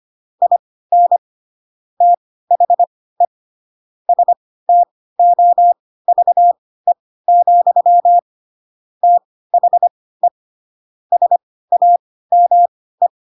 Morse code, heard as a keyed tone, about 25 wpm, spaced as for 16 wpm, 700 Hz. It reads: IN THE STOVE, THE SAME
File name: Pinocchio / Pinocchio ch1 10 25 ik1hge